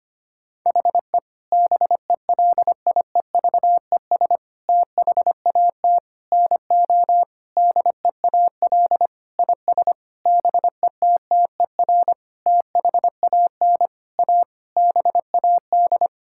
Morse code, heard as a keyed tone, about 25 wpm, 700 Hz. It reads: HE BELIEVES THAT NO DEAL IS BETTER THAN A BAD
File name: practice / 25wpm / Sentences with 9-Word Phrases - 25wpm